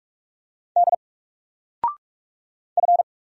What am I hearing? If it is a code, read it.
Morse code, heard as a keyed tone, about 44 wpm, 700 Hz. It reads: D F
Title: Single Letter-Number Alt 45wpm